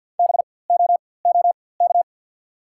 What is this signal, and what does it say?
Morse code, heard as a keyed tone, about 54 wpm, spaced as for 24 wpm, 690 Hz. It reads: BOOK
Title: 4x Character Spacing - Top 300 Words - 50wpm